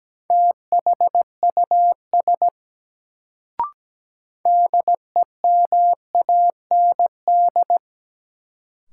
Morse code, heard as a keyed tone, about 17 wpm, 700 Hz. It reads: THUS DEMAND
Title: Sentences from Top 800 Words - New Word Drill - 17wpm